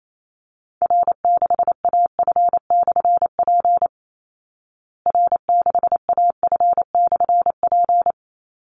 Morse code, heard as a keyed tone, about 28 wpm, 700 Hz. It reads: R6AF/P R6AF/P
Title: Call Signs Repeated 3x - 28wpm